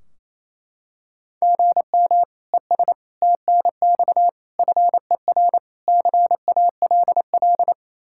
Morse code, heard as a keyed tone, about 28 wpm, 700 Hz. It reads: GM ES TNX FER CALL
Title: Practice QSO - Exchange 1 - Beginning - 28wpm